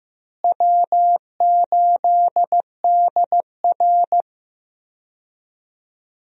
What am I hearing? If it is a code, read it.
Morse code, heard as a keyed tone, about 16 wpm, 700 Hz. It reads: W8DR
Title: USA Call Signs - 4 Characters - Repeated 2x - 15wpm